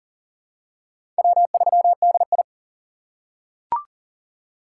Morse code, heard as a keyed tone, about 40 wpm, 700 Hz. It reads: W3DI
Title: USA Call Signs - 4 Characters - 3x Character Spacing - 40wpm